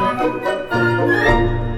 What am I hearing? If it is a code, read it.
There is no code.